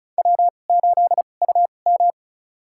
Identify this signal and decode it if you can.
Morse code, heard as a keyed tone, about 36 wpm, spaced as for 25 wpm, 700 Hz. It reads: W8UM
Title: USA Call Signs - 4 Characters - 2x Character Spacing - 35wpm